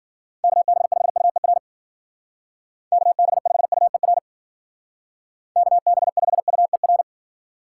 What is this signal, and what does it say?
Morse code, heard as a keyed tone, about 50 wpm, 700 Hz. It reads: KB5UER KB5UER KB5UER
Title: Speed Racing - USA 2x3 Call Signs 40wpm